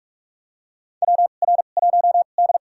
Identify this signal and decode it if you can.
Morse code, heard as a keyed tone, about 47 wpm, spaced as for 33 wpm, 700 Hz. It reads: WR1D
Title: USA Call Signs - 4 Characters - 2x Character Spacing - 45wpm